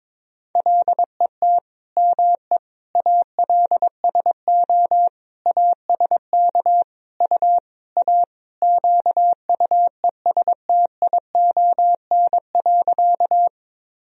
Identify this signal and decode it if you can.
Morse code, heard as a keyed tone, about 22 wpm, 700 Hz. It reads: LET ME ALSO ASK U A QUESTION.
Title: Sentences from Top 400 Words and Abbreviations 22wpm